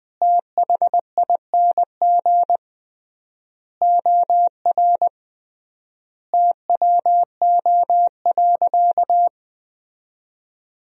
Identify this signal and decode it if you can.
Morse code, heard as a keyed tone, about 20 wpm, 700 Hz. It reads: THING OR TWO.
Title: 3x Word-Spacing - Sentences from Top 300 Words - 20wpm